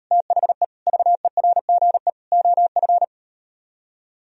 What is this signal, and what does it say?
Morse code, heard as a keyed tone, about 38 wpm, 700 Hz. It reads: THE VERGE OF